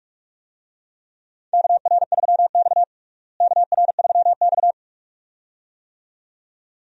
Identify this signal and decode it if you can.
Morse code, heard as a keyed tone, about 46 wpm, 700 Hz. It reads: KR3X KR3X
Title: USA Call Signs - 4 Characters - Repeated 2x - 45wpm